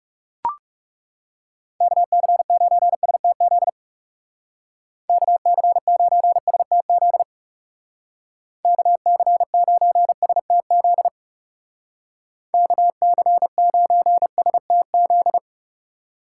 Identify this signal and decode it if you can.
Morse code, heard as a keyed tone, about 37 wpm, 700 Hz. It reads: KC9STZ KC9STZ KC9STZ KC9STZ
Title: Speed Racing - USA 2x3 Call Signs 30wpm